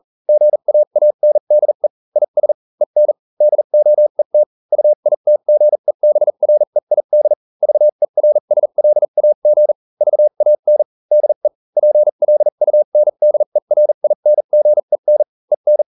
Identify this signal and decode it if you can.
Morse code, heard as a keyed tone, about 40 wpm, spaced as for 35 wpm, 600 Hz. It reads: GAANDE IS EN DOET UITGEBREID VERSLAG VAN DE PLUNDERINGEN EN